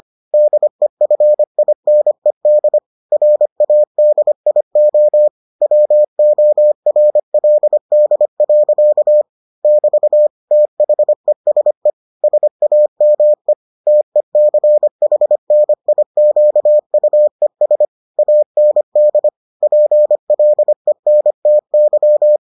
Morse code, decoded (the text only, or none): DEFINED RADIO WORLD. <BT> THESE SAME TECHNIQUES AND PLENTY